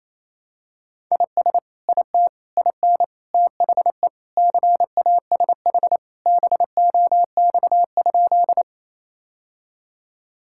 Morse code, heard as a keyed tone, about 28 wpm, 700 Hz. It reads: IS IT IN THE CASH BOX?
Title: Sentences from Top 1000 Words - Review Part 5 - 28wpm